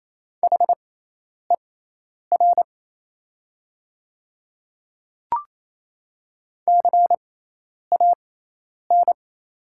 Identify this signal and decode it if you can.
Morse code, heard as a keyed tone, about 27 wpm, spaced as for 10 wpm, 690 Hz. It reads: HER CAN
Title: 6x Character Spacing - Top 100 Words - 28wpm